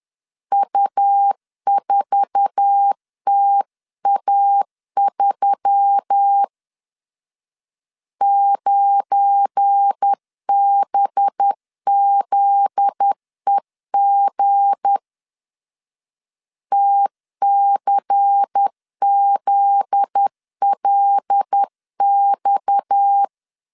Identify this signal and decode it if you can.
Morse code, heard as a keyed tone, about 10 wpm, 800 Hz. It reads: U4TA3 9BZEG TCZLX
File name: LEZIONI / 8 YJ45 / 8 YJ4 10wpm 2633